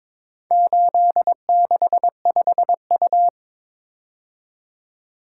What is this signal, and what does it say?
Morse code, heard as a keyed tone, about 21 wpm, 700 Hz. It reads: 865U
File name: practice / 22wpm / Four Letter-Number Combinations - 22wpm